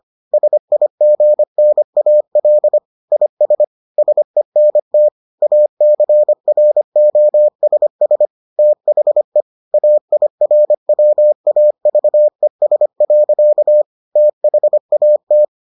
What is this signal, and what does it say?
Morse code, heard as a keyed tone, about 25 wpm, 600 Hz. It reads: SIGNAL IS SENT ACROSS THE AIRWAVES. THAT